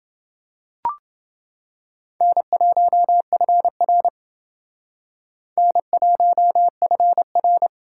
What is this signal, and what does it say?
Morse code, heard as a keyed tone, about 28 wpm, 700 Hz. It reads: N1FR N1FR